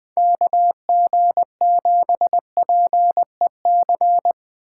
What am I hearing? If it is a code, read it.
Morse code, heard as a keyed tone, about 20 wpm, 700 Hz. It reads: KG7PEC
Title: Speed Racing - USA Call Signs 17wpm